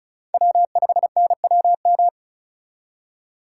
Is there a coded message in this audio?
Yes: Morse code from 0.3 to 2.1 s, heard as a keyed tone, about 44 wpm, 700 Hz. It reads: W5NWM